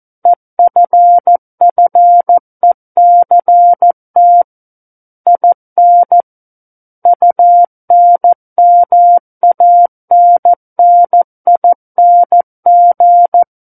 Morse code, heard as a keyed tone, about 14 wpm, 700 Hz. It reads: EFFECT IN UNMANNING
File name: Book One / 05-Chapter 5 - 14 WPM